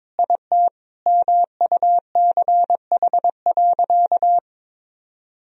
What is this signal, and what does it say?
Morse code, heard as a keyed tone, about 22 wpm, 700 Hz. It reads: IT MUCH.
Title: Sentences from Top 1000 Words - Review Part 1 - 22wpm